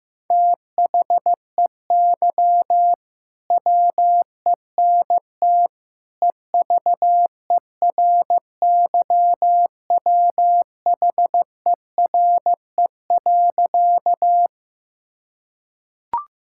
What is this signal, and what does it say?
Morse code, heard as a keyed tone, about 15 wpm, 700 Hz. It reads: THEY WENT EVERYWHERE.
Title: Sentences from Top 1600 Words 15wpm